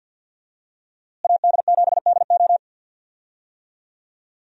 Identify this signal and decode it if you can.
Morse code, heard as a keyed tone, about 51 wpm, 700 Hz. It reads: AD7DO
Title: USA Call Signs - 5 Characters - Repeated 3x - 50wpm